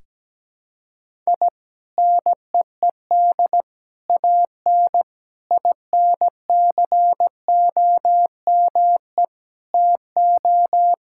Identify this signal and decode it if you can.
Morse code, heard as a keyed tone, about 17 wpm, 700 Hz. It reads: I NEED AN INCOME TO
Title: Sentences from Top 1000 Words - Review Part 5 - 17wpm